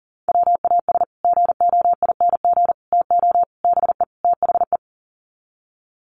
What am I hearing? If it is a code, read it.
Morse code, heard as a keyed tone, about 40 wpm, 700 Hz. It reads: WAS GOING TO BE THE
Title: Sets of 5 Words Encore2 40wpm